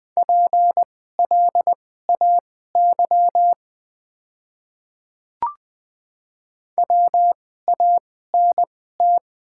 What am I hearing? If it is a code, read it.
Morse code, heard as a keyed tone, about 20 wpm, spaced as for 15 wpm, 700 Hz. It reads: PLAY WANT
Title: 2x Character Spacing - Top 300 Words - 20wpm